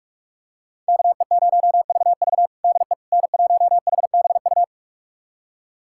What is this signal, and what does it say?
Morse code, heard as a keyed tone, about 46 wpm, 700 Hz. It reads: KE0VV DE N1HBU
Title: USA Callsign DE Callsign - 5 Characters - 45wpm